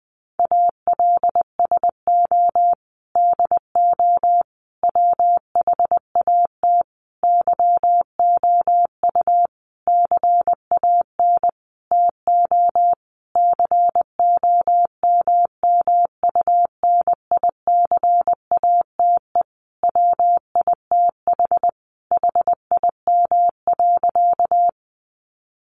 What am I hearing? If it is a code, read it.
Morse code, heard as a keyed tone, about 20 wpm, 700 Hz. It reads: ALSO DO WHAT YOU CAN TO COMMUNICATE WITH HIM.
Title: Sentences from Top 1300 Words 20wpm